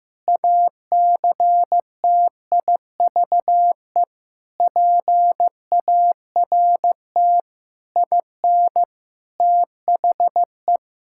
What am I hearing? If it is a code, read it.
Morse code, heard as a keyed tone, about 15 wpm, 700 Hz. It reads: ACTIVE PART IN THE
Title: Sentences from Top 1000 Words - Review Part 6 - 15wpm